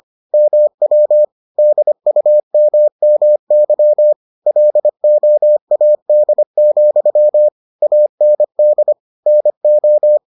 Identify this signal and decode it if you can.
Morse code, heard as a keyed tone, about 25 wpm, 600 Hz. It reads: MW DUMMY LOAD, AND NO